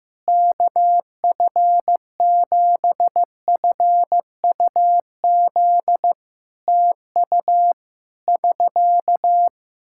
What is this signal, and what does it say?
Morse code, heard as a keyed tone, about 15 wpm, 700 Hz. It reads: KF7FUZ TU <SK>
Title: Practice QSO - Exchange 4 - Ending - 15wpm